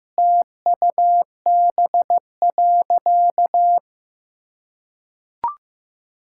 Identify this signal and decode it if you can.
Morse code, heard as a keyed tone, about 15 wpm, 700 Hz. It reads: TUB.
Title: Pangram Challenge v2 - 15wpm